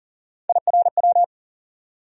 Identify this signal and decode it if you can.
Morse code, heard as a keyed tone, about 40 wpm, 700 Hz. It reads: IRW
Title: USA Call Sign Postfixes - 3 Letters 40wpm